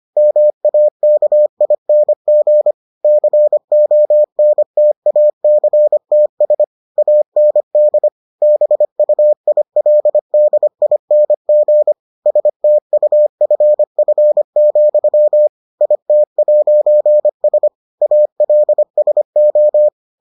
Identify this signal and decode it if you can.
Morse code, heard as a keyed tone, about 25 wpm, 600 Hz. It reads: MAKING CONTACTS AND BUILDING STUFF, IT'S ALSO